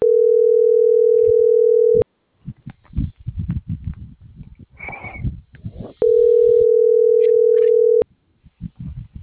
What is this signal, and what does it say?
Ringback tone (North America), 440 + 480 Hz.